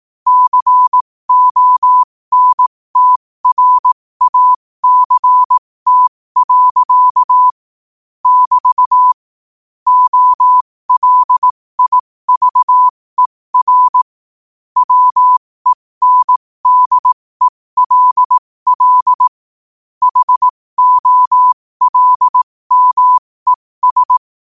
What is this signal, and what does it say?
Morse code, heard as a keyed tone, about 18 wpm, spaced as for 15 wpm, 1000 Hz. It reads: CONTRACT. <BT> OLIVER WENDELL HOLMES
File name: qotdfiles / 15wpm